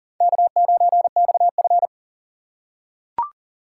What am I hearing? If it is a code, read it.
Morse code, heard as a keyed tone, about 40 wpm, 700 Hz. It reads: K9XF
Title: USA Call Signs - 4 Characters - 6x Character Spacing - 40wpm